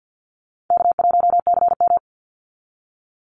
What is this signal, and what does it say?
Morse code, heard as a keyed tone, about 49 wpm, 700 Hz. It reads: K1CM
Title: NA SOTA Chaser Callsigns 50wpm